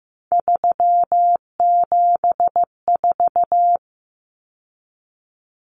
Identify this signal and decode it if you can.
Morse code, heard as a keyed tone, about 15 wpm, 700 Hz. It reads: 374
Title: Numbers - 3-Digit 15wpm